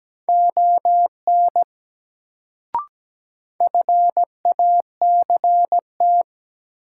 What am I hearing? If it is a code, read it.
Morse code, heard as a keyed tone, about 17 wpm, 700 Hz. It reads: ON FACT
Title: Sets of 2 Words - New Word Drill - 17wpm